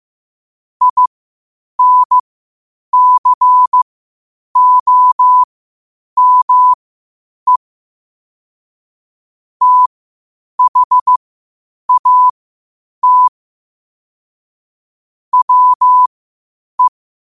Morse code, heard as a keyed tone, about 15 wpm, spaced as for 8 wpm, 1000 Hz. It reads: INCOME THAT WE
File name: qotdfiles / 8wpm